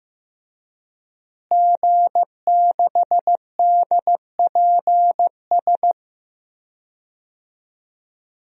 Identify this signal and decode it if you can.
Morse code, heard as a keyed tone, about 15 wpm, 700 Hz. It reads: G6DPS
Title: DXCC-100 Call Signs - 5 Characters - Repeated 2x - 15wpm